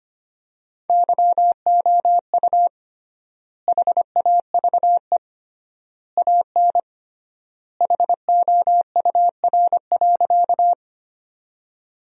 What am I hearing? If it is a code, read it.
Morse code, heard as a keyed tone, about 25 wpm, 700 Hz. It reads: YOU HAVE AN HOUR.
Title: 3x Word-Spacing - Sentences from Top 400 Words - 25wpm